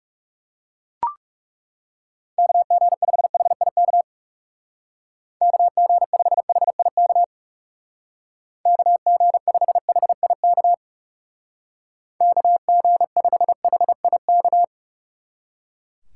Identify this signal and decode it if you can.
Morse code, heard as a keyed tone, about 37 wpm, 700 Hz. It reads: KG5HIK KG5HIK KG5HIK KG5HIK
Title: Speed Racing - USA Call Signs 30wpm